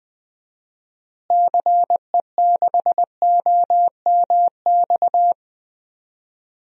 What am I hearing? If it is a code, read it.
Morse code, heard as a keyed tone, about 20 wpm, 700 Hz. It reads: CE6OMX
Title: DXCC-100 Call Signs - 6 Characters - Repeated 2x - 20wpm